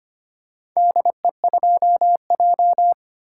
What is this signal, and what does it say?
Morse code, heard as a keyed tone, about 25 wpm, 700 Hz. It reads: DE2J
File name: practice / 25wpm / Four Letter-Number Combinations - 25wpm